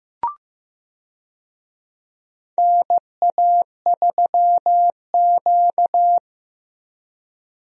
Morse code, heard as a keyed tone, about 15 wpm, 700 Hz. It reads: NA3Q
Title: USA Call Signs - 4 Characters - Repeated 2x - 15wpm